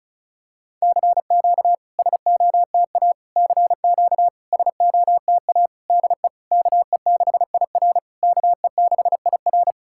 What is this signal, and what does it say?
Morse code, heard as a keyed tone, about 36 wpm, 690 Hz. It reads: CQ SOTA CQ SOTA DE KE6IR KE6IR
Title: SOTA QSO Practice 35wpm